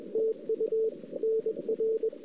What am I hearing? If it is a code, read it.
Morse code, heard as a keyed tone, about 29 wpm, spaced as for 26 wpm, 410 Hz. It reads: TU NR
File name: CW / 2008-ss-cw-k5tr-op-n5rz -05:00